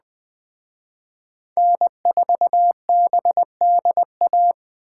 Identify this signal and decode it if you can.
Morse code, heard as a keyed tone, about 21 wpm, 700 Hz. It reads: N4BDA
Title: Speed Racing - USA 1x3 Call Signs 17wpm